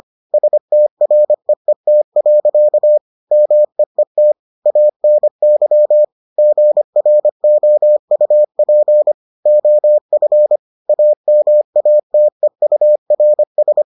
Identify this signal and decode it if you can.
Morse code, heard as a keyed tone, about 25 wpm, 600 Hz. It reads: STREET. MEET ANY GROUP OF AMATEURS